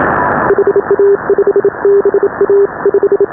Morse code, heard as a keyed tone, about 27 wpm, 400 Hz. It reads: HA5BA 5
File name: AUDIO2019 / HA5BA CW